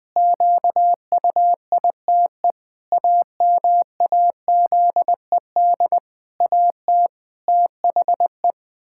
Morse code, heard as a keyed tone, about 21 wpm, 700 Hz. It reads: QUITE AMAZED AT THE
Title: Pangram Challenge v2 - 20wpm